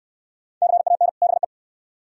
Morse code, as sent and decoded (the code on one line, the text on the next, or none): -.-. .- -. / -... .
CAN BE